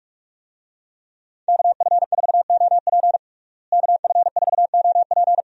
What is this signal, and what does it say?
Morse code, heard as a keyed tone, about 41 wpm, 700 Hz. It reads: KF4OP KF4OP
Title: USA Call Signs - 5 Characters - Repeated 3x - 45wpm